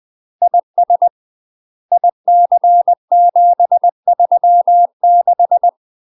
Morse code, heard as a keyed tone, about 20 wpm, 700 Hz. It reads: IS IC736
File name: UK MASTER QSOs TRAINING / 20WPM / UK MASTER QSOs 0024